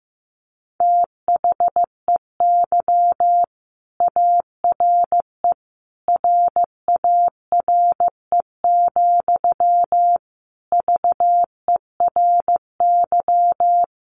Morse code, heard as a keyed tone, about 15 wpm, 700 Hz. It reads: THEY ARE RARE, VERY
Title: Sentences from Top 1100 Words 15wpm